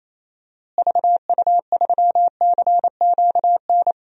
Morse code, heard as a keyed tone, about 28 wpm, 700 Hz. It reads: VU3CQN
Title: DXCC-100 Call Signs - 6 Characters - Repeated 3x - 28wpm